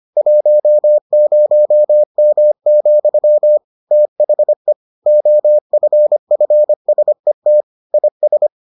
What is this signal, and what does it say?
Morse code, heard as a keyed tone, about 25 wpm, 600 Hz. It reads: 10M, THE OFFSET IS